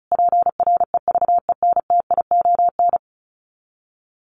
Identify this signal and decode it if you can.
Morse code, heard as a keyed tone, about 34 wpm, 700 Hz. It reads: PREVENTION
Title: Sets of 5 Words Encore2 35wpm